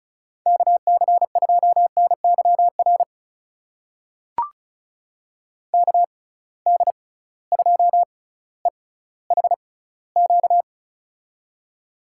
Morse code, heard as a keyed tone, about 36 wpm, spaced as for 18 wpm, 700 Hz. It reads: KC2NYR KD2EHQ
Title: USA Call Signs - 6 Characters - 6x Character Spacing - 35wpm